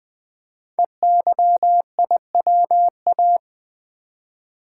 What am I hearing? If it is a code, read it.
Morse code, heard as a keyed tone, about 20 wpm, 700 Hz. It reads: EYIWA
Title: Code Groups - 5 Letters - 20wpm